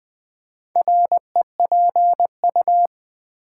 Morse code, heard as a keyed tone, about 20 wpm, 700 Hz. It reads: REPU
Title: Build A Word - 2 Letters At A Time - 8-Letter Words - 20wpm